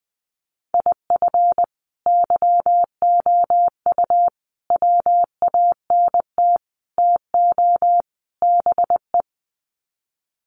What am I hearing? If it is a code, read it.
Morse code, heard as a keyed tone, about 20 wpm, 700 Hz. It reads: IF YOU WANT TO BE
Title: Sets of 5 Words Encore 20wpm